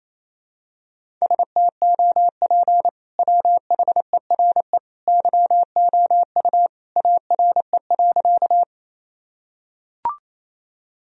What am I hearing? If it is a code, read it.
Morse code, heard as a keyed tone, about 28 wpm, 700 Hz. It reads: STOP WHERE YOU ARE.
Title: Sentences from Top 1000 Words - Review Part 1 - 28wpm